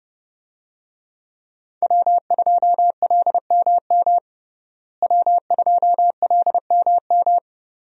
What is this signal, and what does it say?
Morse code, heard as a keyed tone, about 31 wpm, 700 Hz. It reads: W2LMM W2LMM
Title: USA Call Signs - 5 Characters - Repeated 2x - 30wpm